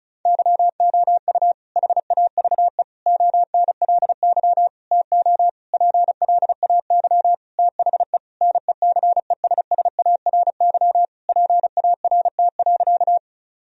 Morse code, heard as a keyed tone, about 36 wpm, 700 Hz. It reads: YOU HAVE ONLY TO PLAY THE NECESSARY PART.
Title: Sentences from Top 1000 Words - Review Part 1 - 35wpm